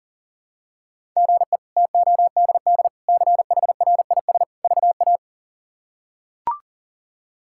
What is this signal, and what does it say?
Morse code, heard as a keyed tone, about 40 wpm, 700 Hz. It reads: GE TODD CHRIS VA
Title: K1USN SST QSO Practice - 40wpm